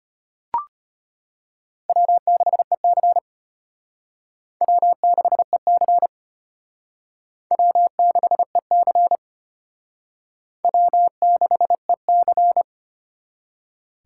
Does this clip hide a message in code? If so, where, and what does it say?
Yes: Morse code from 1.9 to 12.6 s, heard as a keyed tone, about 34 wpm, 700 Hz. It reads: W6EC W6EC W6EC W6EC